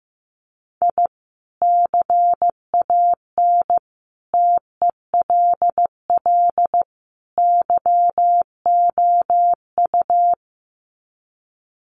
Morse code, heard as a keyed tone, about 15 wpm, 700 Hz. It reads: I CAN TELL YOU
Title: Sets of 4 Words 15wpm